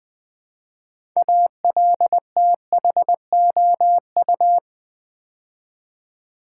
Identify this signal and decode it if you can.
Morse code, heard as a keyed tone, about 20 wpm, 700 Hz. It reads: ALTHOU
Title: Build A Word - 3 Letters At A Time - 8-Letter Words - 20wpm